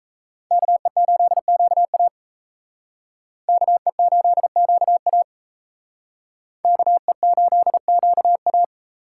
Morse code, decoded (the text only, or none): KE8QA KE8QA KE8QA